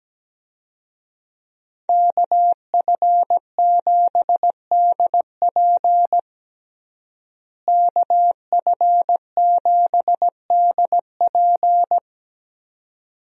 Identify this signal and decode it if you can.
Morse code, heard as a keyed tone, about 17 wpm, 700 Hz. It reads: KF7DP KF7DP
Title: USA Call Signs - 5 Characters - Repeated 2x - 17wpm